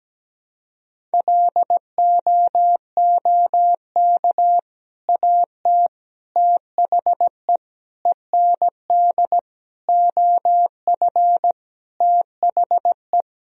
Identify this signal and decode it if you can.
Morse code, heard as a keyed tone, about 17 wpm, 700 Hz. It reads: LOOK AT THE END OF THE
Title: Sentences from Top 1000 Words - Review Part 1 - 17wpm